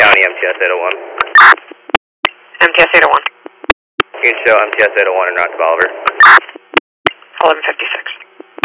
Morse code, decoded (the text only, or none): none